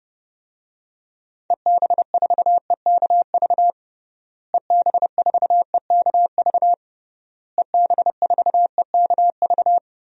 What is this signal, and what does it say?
Morse code, heard as a keyed tone, about 30 wpm, 700 Hz. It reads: EB4EKV EB4EKV EB4EKV
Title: DXCC-100 Call Signs - 6 Characters - Repeated 3x - 30wpm